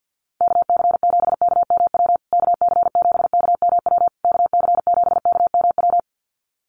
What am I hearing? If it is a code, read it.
Morse code, heard as a keyed tone, about 50 wpm, 700 Hz. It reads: KC7KMW KC7KMW KC7KMW
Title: USA Call Signs Repeated 3x 50wpm